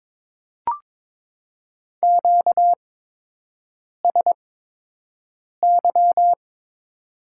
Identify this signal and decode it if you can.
Morse code, heard as a keyed tone, about 22 wpm, spaced as for 6 wpm, 700 Hz. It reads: QSY
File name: Top 90 QSO Elements - 8x Character Spacing - 22wpm